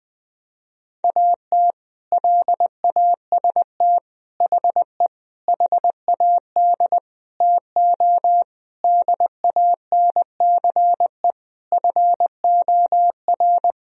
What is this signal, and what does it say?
Morse code, heard as a keyed tone, about 21 wpm, 700 Hz. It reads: AT LAST HE HAD TO DANCE FOR
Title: Sentences from Top 1000 Words - Review Part 4 - 20wpm